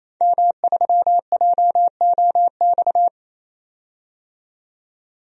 Morse code, heard as a keyed tone, about 28 wpm, 700 Hz. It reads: M3JOX
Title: DXCC-100 Call Signs - 5 Characters - 28wpm